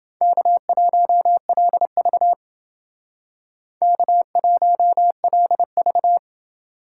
Morse code, heard as a keyed tone, about 28 wpm, 700 Hz. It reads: K1LV K1LV